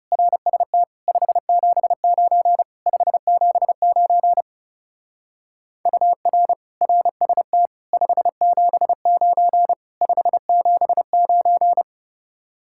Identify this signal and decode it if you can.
Morse code, heard as a keyed tone, about 35 wpm, 700 Hz. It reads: RST 579 579 UR RST 579 579
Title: Speed-Racing Common QSO Phrases Encore - 30wpm